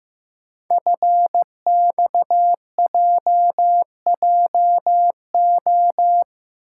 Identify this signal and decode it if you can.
Morse code, heard as a keyed tone, about 15 wpm, 700 Hz. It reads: FXJJO
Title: Code Groups - 5 Letters - 15wpm